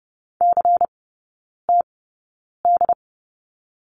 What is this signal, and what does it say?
Morse code, heard as a keyed tone, about 30 wpm, spaced as for 9 wpm, 700 Hz. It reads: CTD
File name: Lesson 16e - Letter c - Three Characters 30wpm